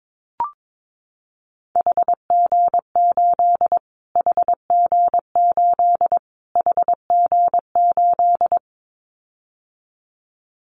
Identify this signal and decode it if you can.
Morse code, heard as a keyed tone, about 22 wpm, 700 Hz. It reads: HG8 HG8 HG8
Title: Callsign Prefixes for Top 25 DXCC - Repeated 3x - 22wpm